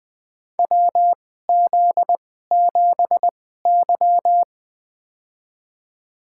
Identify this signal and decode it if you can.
Morse code, heard as a keyed tone, about 20 wpm, spaced as for 14 wpm, 700 Hz. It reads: WZ7Y
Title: USA Call Signs - 4 Characters - 2x Character Spacing - 20wpm